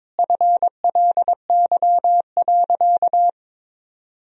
Morse code, heard as a keyed tone, about 22 wpm, 700 Hz. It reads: FLY.
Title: Sentences from Top 1000 Words - Review Part 2 - 22wpm